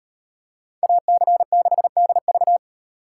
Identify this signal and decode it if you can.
Morse code, heard as a keyed tone, about 38 wpm, 700 Hz. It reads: AC6DV